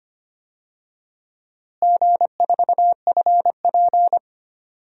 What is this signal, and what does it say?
Morse code, heard as a keyed tone, about 25 wpm, 700 Hz. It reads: G4FP
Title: DXCC-100 Call Signs - 4 Characters - Repeated 2x - 25wpm